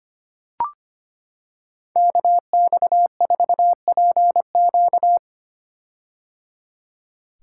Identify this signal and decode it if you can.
Morse code, heard as a keyed tone, about 25 wpm, 700 Hz. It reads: KX4PQ
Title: K1USN SST QSO Practice - 25wpm